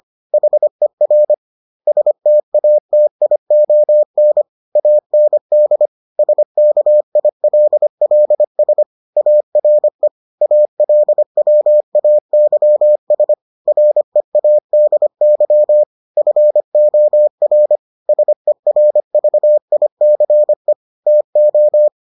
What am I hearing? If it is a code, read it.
Morse code, heard as a keyed tone, about 25 wpm, 600 Hz. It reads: HER STATION AND SKILLS ARE ALWAYS READY FOR SERVICE TO